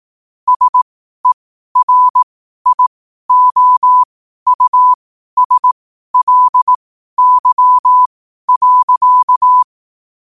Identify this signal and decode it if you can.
Morse code, heard as a keyed tone, about 18 wpm, spaced as for 12 wpm, 1000 Hz. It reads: SERIOUSLY.
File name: qotdfiles / 12wpm